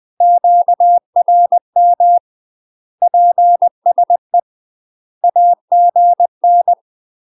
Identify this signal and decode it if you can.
Morse code, heard as a keyed tone, about 20 wpm, 700 Hz. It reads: QRM PSE AGN
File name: UK MASTER QSOs TRAINING / 20WPM / UK MASTER QSOs 0025